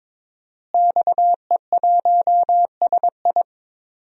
Morse code, heard as a keyed tone, about 23 wpm, 700 Hz. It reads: XE1SI
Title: DXCC-100 Call Signs - 5 Characters - Repeated 3x - 22wpm